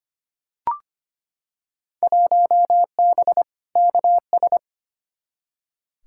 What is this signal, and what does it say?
Morse code, heard as a keyed tone, about 25 wpm, 700 Hz. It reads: 1B KS
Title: Field Day Exchanges Encore - 25wpm